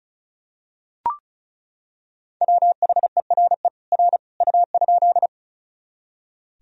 Morse code, heard as a keyed tone, about 34 wpm, 700 Hz. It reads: WHERE R U?
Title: Sentences from Top 200 Words and Abbreviations 35wpm